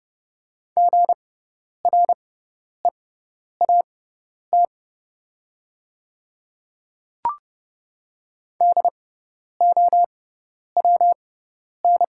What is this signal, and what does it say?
Morse code, heard as a keyed tone, about 30 wpm, spaced as for 10 wpm, 700 Hz. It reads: GREAT DOWN
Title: 6x Character Spacing - Top 200 Words - 30wpm